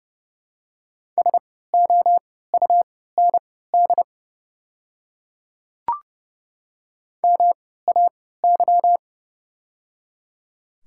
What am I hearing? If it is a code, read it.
Morse code, heard as a keyed tone, about 32 wpm, spaced as for 17 wpm, 700 Hz. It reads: SOUND MAY
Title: 3x Character Spacing - Top 100 Words - 30wpm